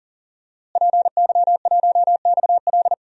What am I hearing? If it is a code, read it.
Morse code, heard as a keyed tone, about 38 wpm, 700 Hz. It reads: PY1XL
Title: DXCC-100 Call Signs - 5 Characters - Repeated 2x - 40wpm